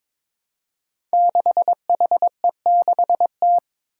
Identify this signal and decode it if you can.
Morse code, heard as a keyed tone, about 22 wpm, 700 Hz. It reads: 6HE6T
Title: Five Challenging Letter-Number Combinations 22wpm